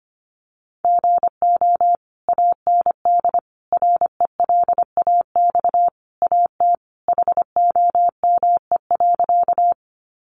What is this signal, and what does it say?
Morse code, heard as a keyed tone, about 25 wpm, 700 Hz. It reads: GO AND RELAX AT HOME.
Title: Sentences from Top 1400 Words 25wpm